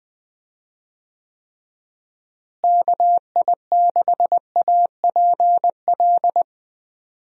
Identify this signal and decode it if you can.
Morse code, heard as a keyed tone, about 21 wpm, 700 Hz. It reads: KI6APL
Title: USA Call Signs - 6 Characters - Repeated 3x - 20wpm